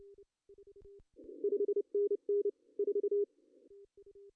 Morse code, heard as a keyed tone, about 28 wpm, 410 Hz. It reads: 5NN 4
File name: CW / cqww-cw-k5tr